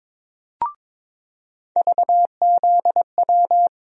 Morse code, heard as a keyed tone, about 21 wpm, 700 Hz. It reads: VZW
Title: USA Call Sign Postfixes - 3 Letters 22wpm